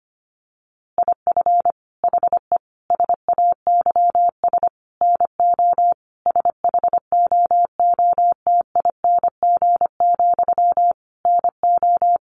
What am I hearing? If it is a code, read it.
Morse code, heard as a keyed tone, about 25 wpm, 700 Hz. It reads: IF HE SAYS NO SHOOTING, NO
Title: A Sound Of Thunder 25wpm